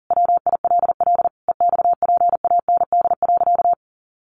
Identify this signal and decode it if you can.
Morse code, heard as a keyed tone, about 40 wpm, 700 Hz. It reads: WILL EXPAND.
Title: Sentences from Top 1400 Words 40wpm